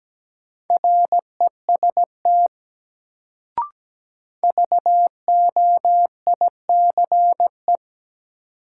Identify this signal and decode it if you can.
Morse code, heard as a keyed tone, about 17 wpm, 700 Hz. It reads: REST VOICE
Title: Sentences from Top 500 Words - New Word Drill - 17wpm